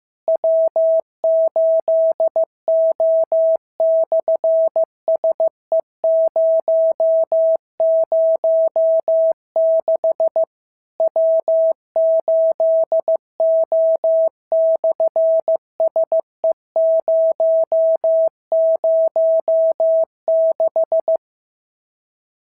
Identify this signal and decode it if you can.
Morse code, heard as a keyed tone, about 15 wpm, 650 Hz. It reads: W8O/SE006 W8O/SE006
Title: SOTA QSO Practice 15wpm